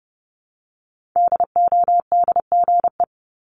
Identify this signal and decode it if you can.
Morse code, heard as a keyed tone, about 30 wpm, 700 Hz. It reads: DODGE
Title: Multi-letter Phonemes Repeated 3x 30wpm